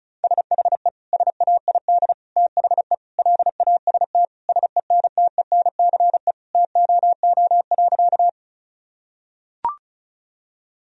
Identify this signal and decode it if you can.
Morse code, heard as a keyed tone, about 34 wpm, 700 Hz. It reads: SHE SAID THE LAST SENTENCE TOO.
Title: Sentences from Top 1000 Words - Review Part 1 - 35wpm